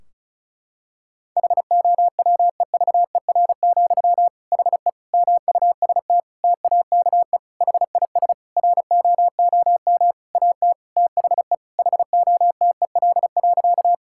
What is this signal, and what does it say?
Morse code, heard as a keyed tone, about 36 wpm, 700 Hz. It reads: HOWEVER, HE MUST TAKE HIS ROOM AT THE HOTEL.
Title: Sentences from Top 1600 Words 35wpm